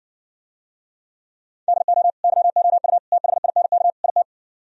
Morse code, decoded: DO YOU THINK IT

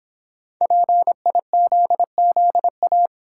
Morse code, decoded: PIZZA

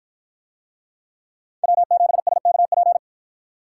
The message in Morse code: W7SKP

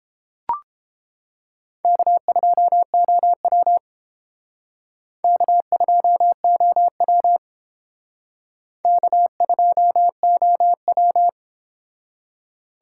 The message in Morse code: K2OW K2OW K2OW

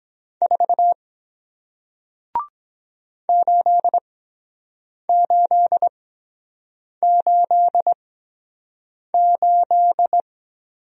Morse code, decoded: 4 8888

